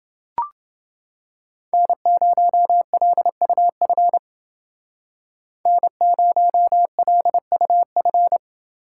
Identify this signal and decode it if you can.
Morse code, heard as a keyed tone, about 29 wpm, 700 Hz. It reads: N0LUF N0LUF